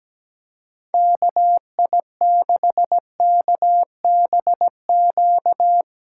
Morse code, heard as a keyed tone, about 19 wpm, 700 Hz. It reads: KI6KBQ
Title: Speed Racing - USA 2x3 Call Signs 17wpm